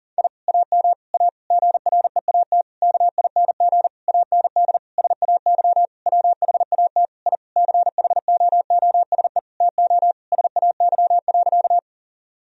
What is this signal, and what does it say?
Morse code, heard as a keyed tone, about 40 wpm, 700 Hz. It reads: I AM A GREAT KING AND SAY WHAT I CHOOSE TO SAY.
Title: Sentences from Top 1000 Words - Review Part 4 - 40wpm